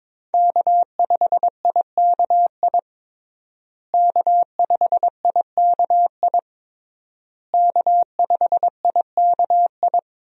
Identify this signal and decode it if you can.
Morse code, heard as a keyed tone, about 22 wpm, 700 Hz. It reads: K5IKI K5IKI K5IKI
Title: USA Call Signs - 5 Characters - Repeated 3x - 22wpm